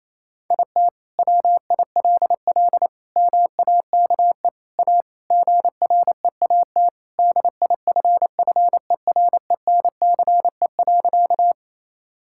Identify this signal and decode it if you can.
Morse code, heard as a keyed tone, about 28 wpm, 700 Hz. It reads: IT WILL MAKE A GREAT DIFFERENCE.
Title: Sentences from Top 1000 Words - Review Part 5 - 28wpm